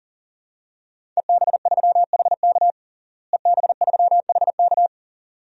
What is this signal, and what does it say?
Morse code, heard as a keyed tone, about 37 wpm, 700 Hz. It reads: EB3HK EB3HK
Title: DXCC-100 Call Signs - 5 Characters - Repeated 2x - 40wpm